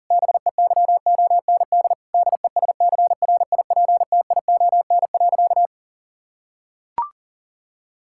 Morse code, heard as a keyed tone, about 40 wpm, 700 Hz. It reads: BEYOND DESCRIPTION.
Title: Sentences from Top 1600 Words 40wpm